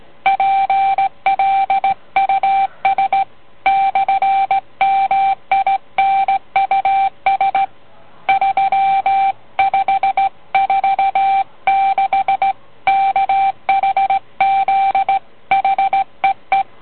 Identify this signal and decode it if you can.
Morse code, heard as a keyed tone, about 17 wpm, 780 Hz. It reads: PLUS /MINUS 3546 KHZ HEE